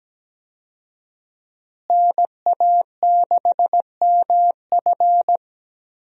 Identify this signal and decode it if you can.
Morse code, heard as a keyed tone, about 17 wpm, 700 Hz. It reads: NA6MF
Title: USA Call Signs - 5 Characters - Repeated 2x - 17wpm